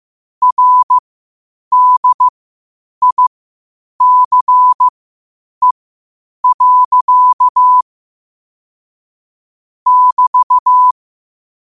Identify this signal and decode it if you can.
Morse code, heard as a keyed tone, about 15 wpm, spaced as for 8 wpm, 1000 Hz. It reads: RDICE. <BT>
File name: qotdfiles / 8wpm